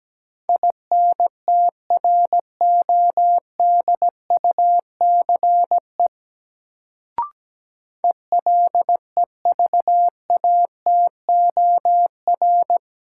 Morse code, heard as a keyed tone, about 17 wpm, 700 Hz. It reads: INTRODUCE ELEVATOR
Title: Sentences from Top 1600 Words - New Word Drill Alt - 17wpm